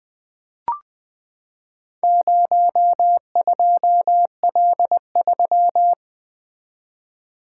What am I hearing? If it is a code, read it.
Morse code, heard as a keyed tone, about 20 wpm, 700 Hz. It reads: 02L3
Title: Four Letter-Number Combinations - 20wpm